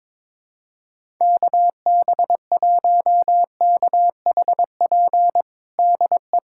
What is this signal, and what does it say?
Morse code, heard as a keyed tone, about 21 wpm, 700 Hz. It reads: KB1KHP DE
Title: USA Callsign DE Callsign 22wpm